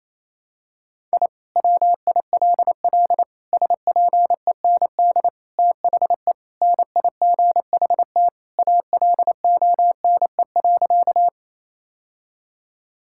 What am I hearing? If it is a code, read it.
Morse code, heard as a keyed tone, about 28 wpm, 700 Hz. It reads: I WILL SPEND THE NIGHT ALONE.
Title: Sentences from Top 1000 Words - Review Part 5 - 28wpm